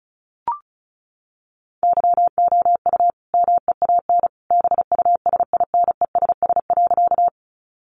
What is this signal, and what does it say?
Morse code, heard as a keyed tone, about 34 wpm, 700 Hz. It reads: YOU MEAN BUSINESS.
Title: Sentences from Top 600 Words 35wpm